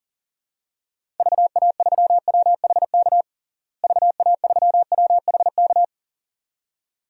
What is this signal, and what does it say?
Morse code, heard as a keyed tone, about 40 wpm, 700 Hz. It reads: VA3WHK VA3WHK
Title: DXCC-100 Call Signs - 6 Characters - Repeated 2x - 40wpm